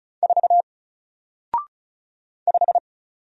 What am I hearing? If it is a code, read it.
Morse code, heard as a keyed tone, about 35 wpm, 700 Hz. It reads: 4 5
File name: Single-Letter-Number-35wpm